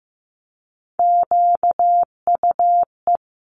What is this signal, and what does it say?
Morse code, heard as a keyed tone, about 15 wpm, 700 Hz. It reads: QUE
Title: Multi-letter Phonemes 15wpm